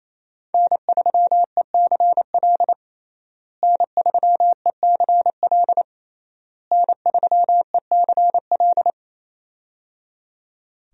Morse code, heard as a keyed tone, about 28 wpm, 700 Hz. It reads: N3ECL N3ECL N3ECL
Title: USA Call Signs - 5 Characters - Repeated 3x - 28wpm